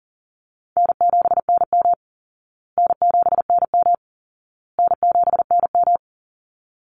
Morse code, heard as a keyed tone, about 40 wpm, 700 Hz. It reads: N7NM N7NM N7NM
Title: Easy Call Signs Repeated 3x - 40wpm